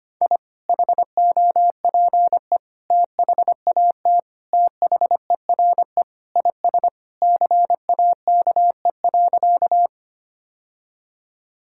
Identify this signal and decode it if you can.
Morse code, heard as a keyed tone, about 25 wpm, 690 Hz. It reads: I HOPE THAT THERE IS CAKE.
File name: practice / 25wpm / Sentences with 2-Word Phrases - 25wpm